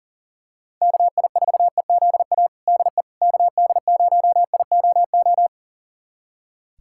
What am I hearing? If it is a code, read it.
Morse code, heard as a keyed tone, about 37 wpm, 700 Hz. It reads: KI4EZA DE KD0IOO
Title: USA Callsign DE Callsign - 6 Characters - 40wpm